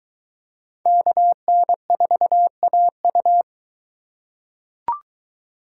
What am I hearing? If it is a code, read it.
Morse code, heard as a keyed tone, about 23 wpm, 700 Hz. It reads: KN4AU